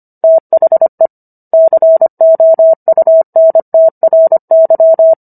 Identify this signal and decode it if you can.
Morse code, heard as a keyed tone, about 25 wpm, 650 Hz. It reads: THE COUNTRY